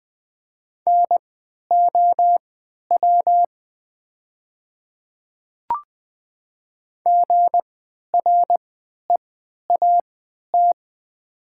Morse code, heard as a keyed tone, about 20 wpm, spaced as for 11 wpm, 690 Hz. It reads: NOW GREAT
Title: 3x Character Spacing - Top 500 Words - 20wpm